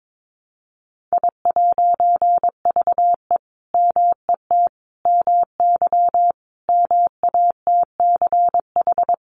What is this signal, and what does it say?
Morse code, heard as a keyed tone, about 22 wpm, 700 Hz. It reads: I'VE MET MY MATCH
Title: A Sound Of Thunder 22wpm